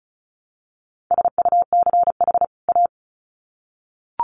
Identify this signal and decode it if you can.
Morse code, heard as a keyed tone, about 34 wpm, 700 Hz. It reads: SUCH A
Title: Sets of 2 Words 35wpm